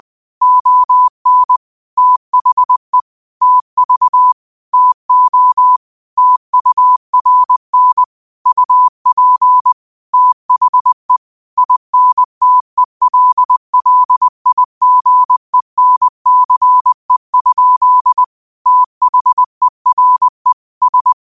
Morse code, heard as a keyed tone, about 20 wpm, 1000 Hz. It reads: ON THE TV TO TURN UP THE INTELLIGENCE? THERE S
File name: qotdfiles / 20wpm